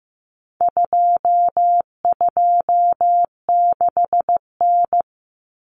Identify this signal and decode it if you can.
Morse code, heard as a keyed tone, about 15 wpm, 700 Hz. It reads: 226N
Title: Common Cut Numbers - 4-Digit 15wpm